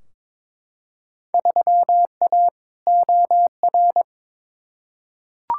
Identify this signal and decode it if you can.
Morse code, heard as a keyed tone, about 22 wpm, 700 Hz. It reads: 3A OR
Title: Field Day Exchanges - 22wpm